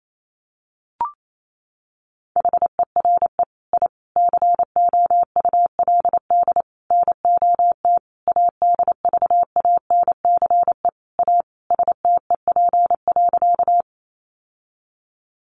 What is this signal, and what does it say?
Morse code, heard as a keyed tone, about 28 wpm, 700 Hz. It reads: HERE I COULD NOT ADVANCE A STEP.
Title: Sentences from Top 1400 Words 28wpm